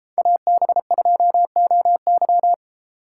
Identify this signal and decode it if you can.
Morse code, heard as a keyed tone, about 31 wpm, 700 Hz. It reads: AB2OY